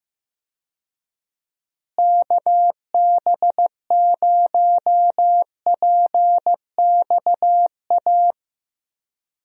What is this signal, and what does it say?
Morse code, heard as a keyed tone, about 15 wpm, 700 Hz. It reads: KB0PXA
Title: USA Call Signs - 6 Characters - Repeated 3x - 15wpm